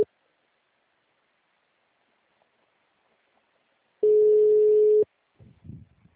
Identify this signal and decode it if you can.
Ringback tone (Europe, 425 Hz).